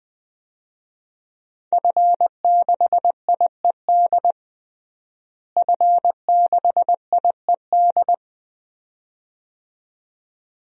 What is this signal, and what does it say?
Morse code, heard as a keyed tone, about 21 wpm, 700 Hz. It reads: F6IED F6IED
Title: DXCC-100 Call Signs - 5 Characters - Repeated 2x - 20wpm